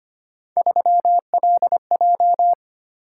Morse code, heard as a keyed tone, about 25 wpm, 700 Hz. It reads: 3LJ
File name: Three Letter-Number Combinations - 25wpm